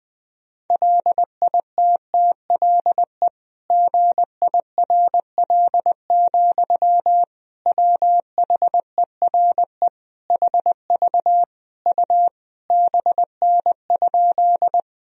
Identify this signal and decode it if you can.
Morse code, heard as a keyed tone, about 20 wpm, 700 Hz. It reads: LITTLE GIRL, WHERE HV U BN?
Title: Sentences from Top 400 Words and Abbreviations 20wpm